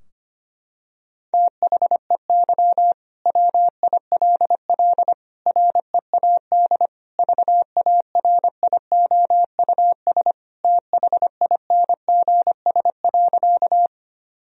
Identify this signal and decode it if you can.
Morse code, heard as a keyed tone, about 25 wpm, 700 Hz. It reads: THEY WILL READ VARIOUS THINGS.
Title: Sentences from Top 1000 Words - Review Part 4 - 25wpm